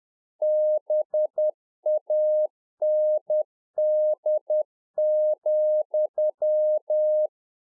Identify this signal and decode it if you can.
Morse code, heard as a keyed tone, about 10 wpm, 610 Hz. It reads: BAND,